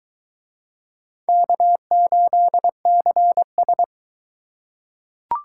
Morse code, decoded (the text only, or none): K8CS